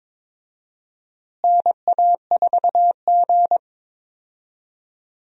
Morse code, heard as a keyed tone, about 22 wpm, 700 Hz. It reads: NA4G
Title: USA Call Signs - 4 Characters - Repeated 3x - 22wpm